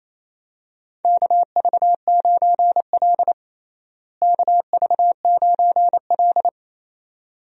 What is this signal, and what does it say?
Morse code, heard as a keyed tone, about 28 wpm, 700 Hz. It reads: KV9L KV9L
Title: USA Call Signs - 4 Characters - Repeated 2x - 28wpm